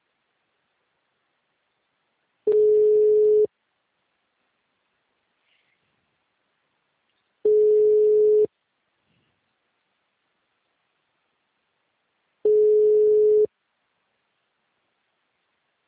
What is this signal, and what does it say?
Ringback tone (Europe, 425 Hz).
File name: out- -2016- - - .81